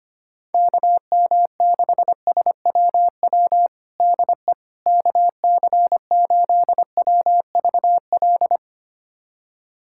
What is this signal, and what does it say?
Morse code, heard as a keyed tone, about 25 wpm, 700 Hz. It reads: KM6SWW DE KC8WVL
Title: USA Callsign DE Callsign 25wpm